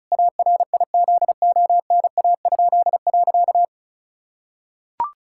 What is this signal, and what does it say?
Morse code, heard as a keyed tone, about 36 wpm, 700 Hz. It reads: ARIZONA?.
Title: Pangram Challenge v2 - 35wpm